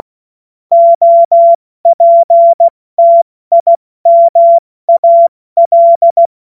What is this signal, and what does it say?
Morse code, heard as a keyed tone, about 16 wpm, 680 Hz. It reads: OPTIMAL
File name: Morsekurs / Witze / Witz 07. Dezember 2021, 70 F80